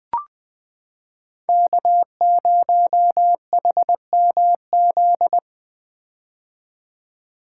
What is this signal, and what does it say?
Morse code, heard as a keyed tone, about 21 wpm, 700 Hz. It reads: K0HMZ
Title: K1USN SST QSO Practice - 20wpm